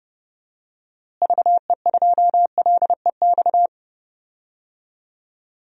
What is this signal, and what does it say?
Morse code, heard as a keyed tone, about 30 wpm, 700 Hz. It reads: VE2LEX
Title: DXCC-100 Call Signs - 6 Characters - Repeated 2x - 30wpm